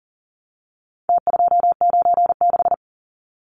Morse code, heard as a keyed tone, about 40 wpm, 700 Hz. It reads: T296
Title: Numbers - Contest Style - 4-Digit 40wpm